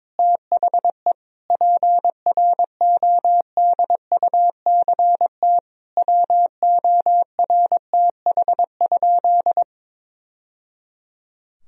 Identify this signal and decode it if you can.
Morse code, heard as a keyed tone, about 21 wpm, 700 Hz. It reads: THE PRODUCT WORTH?
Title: Sentences from Top 1000 Words - Review Part 2 - 22wpm